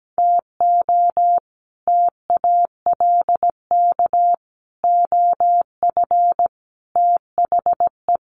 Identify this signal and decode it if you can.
Morse code, heard as a keyed tone, about 17 wpm, 700 Hz. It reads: TO TALK OF THE
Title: Sets of 5 Words 17wpm